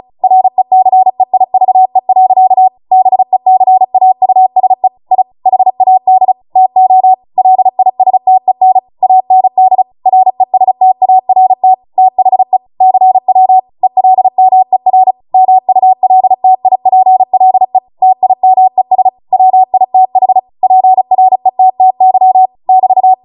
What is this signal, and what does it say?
Morse code, heard as a keyed tone, about 36 wpm, 750 Hz. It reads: RECEIVE. BECAUSE I HAD TO LISTEN AND RESTART THE CW ELMER MULTIPLE TIMES WITH PRETTY <BT>